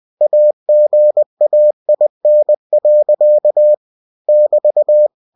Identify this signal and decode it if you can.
Morse code, heard as a keyed tone, about 20 wpm, 600 Hz. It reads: AGAIN. <BT>
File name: cw-books / winnie-the-pooh / winnie-20wpm-page-0018